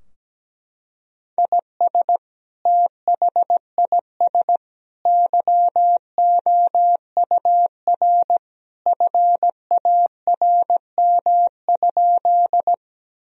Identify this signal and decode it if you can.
Morse code, heard as a keyed tone, about 17 wpm, 700 Hz. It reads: IS THIS YOUR FARM?
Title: Sentences from Top 1000 Words - Review Part 2 - 17wpm